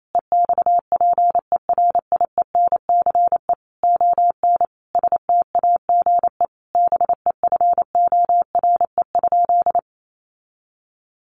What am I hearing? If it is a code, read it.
Morse code, heard as a keyed tone, about 28 wpm, 700 Hz. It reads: EXPERIENCE ON STAGE BEFORE?
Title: Sentences from Top 1200 Words 28wpm